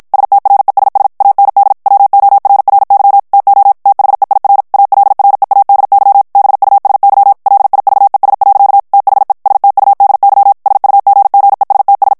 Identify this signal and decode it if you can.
Morse code, heard as a keyed tone, about 51 wpm, 800 Hz. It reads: STRESS AND WORRY TO THEIR ALREADY BUSY LIVES. THE STUDY SUGGESTS
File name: morse mp3 / ANGLAIS texte clair 1 vitesse 50 mots minute